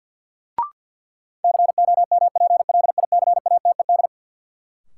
Morse code, heard as a keyed tone, about 53 wpm, 700 Hz. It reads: COMPLICATED
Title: Sentences from Top 1200 Words - New Word Drill - 50wpm